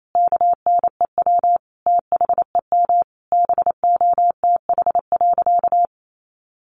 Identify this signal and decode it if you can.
Morse code, heard as a keyed tone, about 28 wpm, 700 Hz. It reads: KNEW THEM BOTH.
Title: Sentences from Top 400 Words 28wpm